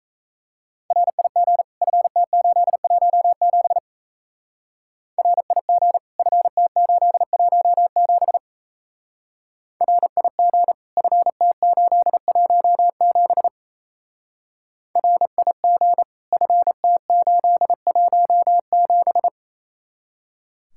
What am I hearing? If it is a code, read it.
Morse code, heard as a keyed tone, about 35 wpm, 700 Hz. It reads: RIG FT817 RIG FT817 RIG FT817 RIG FT817